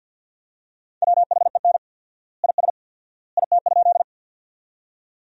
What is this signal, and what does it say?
Morse code, heard as a keyed tone, about 50 wpm, 700 Hz. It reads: WHEN IS IT?
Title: 4x Word-Spacing - Sentences from Top 100 Words - 50wpm